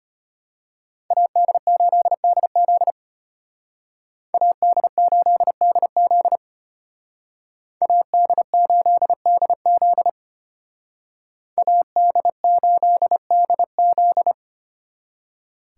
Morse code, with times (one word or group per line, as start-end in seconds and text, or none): 1.1-2.9 AD8DZ
4.3-6.4 AD8DZ
7.8-10.1 AD8DZ
11.6-14.3 AD8DZ